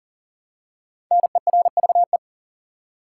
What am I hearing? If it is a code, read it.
Morse code, heard as a keyed tone, about 40 wpm, 700 Hz. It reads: NERVE
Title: Five Letter Words - 40wpm